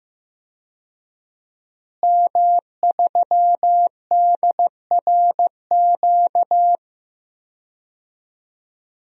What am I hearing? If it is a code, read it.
Morse code, heard as a keyed tone, about 15 wpm, 700 Hz. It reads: M3DRQ
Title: DXCC-100 Call Signs - 5 Characters - Repeated 2x - 15wpm